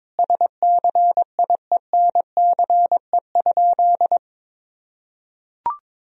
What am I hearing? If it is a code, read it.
Morse code, heard as a keyed tone, about 22 wpm, 700 Hz. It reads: SCIENCE?
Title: Sentences from Top 1000 Words - Review Part 2 - 22wpm